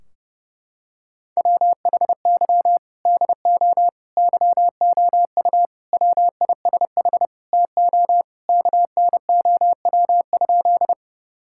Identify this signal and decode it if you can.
Morse code, heard as a keyed tone, about 30 wpm, 700 Hz. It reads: WHY DO YOU WISH TO KNOW?
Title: Sentences from Top 1600 Words 30wpm